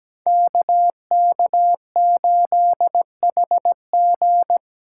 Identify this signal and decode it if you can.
Morse code, heard as a keyed tone, about 17 wpm, 700 Hz. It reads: KK8HG
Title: Code Groups - 5 Letter:Numbers - 17wpm